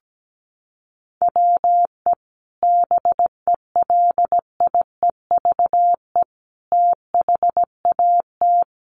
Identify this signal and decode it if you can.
Morse code, heard as a keyed tone, about 17 wpm, 700 Hz. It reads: WE BELIEVE THAT
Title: Sentences from Top 1300 Words 17wpm